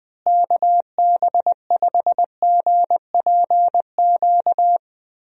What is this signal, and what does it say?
Morse code, heard as a keyed tone, about 20 wpm, 700 Hz. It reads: KB5GPQ
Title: USA Call Signs - 6 Characters - Repeated 3x - 20wpm